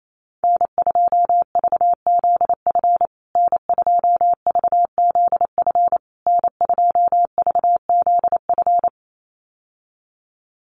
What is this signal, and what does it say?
Morse code, heard as a keyed tone, about 28 wpm, 700 Hz. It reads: N2VZF N2VZF N2VZF
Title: USA Call Signs Repeated 3x 28wpm